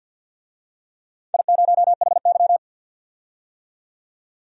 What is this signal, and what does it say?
Morse code, heard as a keyed tone, about 52 wpm, 700 Hz. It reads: I0HY
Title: DXCC-100 Call Signs - 4 Characters - 50wpm